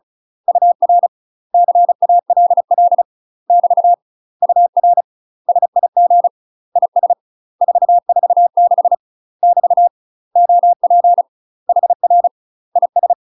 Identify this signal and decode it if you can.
Morse code, heard as a keyed tone, about 36 wpm, 700 Hz. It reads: UR CALL <BT> UR SIG IS 446 <BT> OP HR IS
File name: UK MASTER QSOs TRAINING / 35WPM / UK MASTER QSOs 0031